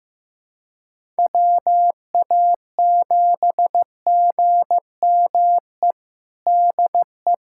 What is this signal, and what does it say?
Morse code, heard as a keyed tone, about 15 wpm, 700 Hz. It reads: WA7GME DE
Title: USA Callsign DE Callsign 15wpm